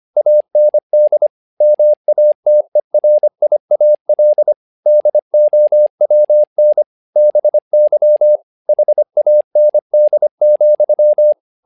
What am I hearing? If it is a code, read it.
Morse code, heard as a keyed tone, about 25 wpm, 600 Hz. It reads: AND MATERIAL DOWN BY HAND,
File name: learning center / tower trouble / tower trouble 25wpm